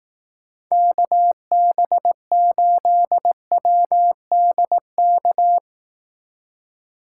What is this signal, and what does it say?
Morse code, heard as a keyed tone, about 18 wpm, 700 Hz. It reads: KB8WDK
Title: Speed Racing - USA 2x3 Call Signs 15wpm